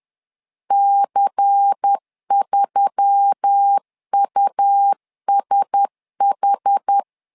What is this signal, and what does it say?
Morse code, heard as a keyed tone, about 10 wpm, 800 Hz. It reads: C3USH
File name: LEZIONI / 5 LBVC / 5 LBVC 10wpm 2349